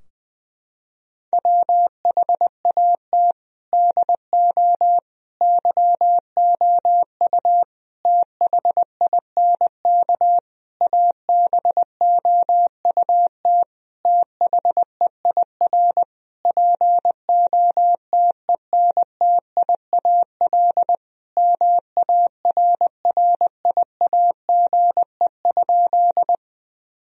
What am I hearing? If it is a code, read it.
Morse code, heard as a keyed tone, about 20 wpm, 700 Hz. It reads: WHAT DO YOU THINK ABOUT THEIR POTENTIAL MARRIAGE?
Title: Sentences from Top 1000 Words - Review Part 4 - 20wpm